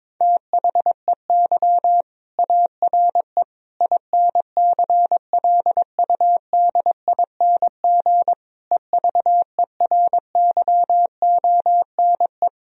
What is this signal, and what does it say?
Morse code, heard as a keyed tone, about 22 wpm, 700 Hz. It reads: THEY ARE INCLUDING EVERYONE
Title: Sentences from Top 1000 Words - Review Part 4 - 22wpm